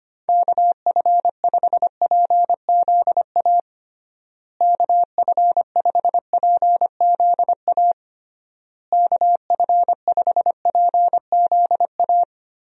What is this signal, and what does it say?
Morse code, heard as a keyed tone, about 25 wpm, 700 Hz. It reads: KF5PZA KF5PZA KF5PZA
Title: USA Call Signs - 6 Characters - Repeated 3x - 25wpm